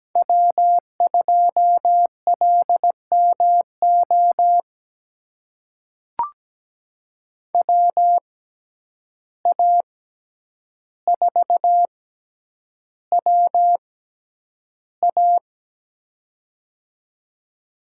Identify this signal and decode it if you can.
Morse code, heard as a keyed tone, about 17 wpm, spaced as for 9 wpm, 700 Hz. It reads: W2LMO WA4WA
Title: USA Call Signs - 5 Characters - 6x Character Spacing - 17wpm